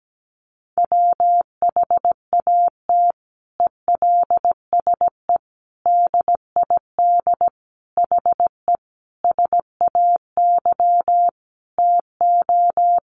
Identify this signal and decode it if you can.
Morse code, heard as a keyed tone, about 17 wpm, 700 Hz. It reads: WHAT ELSE DID HE SAY TO
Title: Sentences from Top 800 Words 17wpm